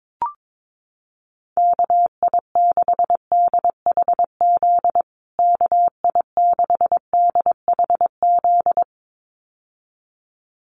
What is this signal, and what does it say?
Morse code, heard as a keyed tone, about 22 wpm, 700 Hz. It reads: KI6DHZ KI6DHZ
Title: USA Call Signs Repeated 2x 22wpm